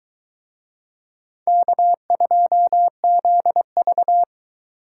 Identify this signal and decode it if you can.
Morse code, heard as a keyed tone, about 24 wpm, 700 Hz. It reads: K2ZV